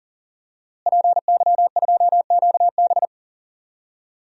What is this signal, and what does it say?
Morse code, heard as a keyed tone, about 39 wpm, 700 Hz. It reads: PY2QB
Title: DXCC-100 Call Signs - 5 Characters - Repeated 2x - 40wpm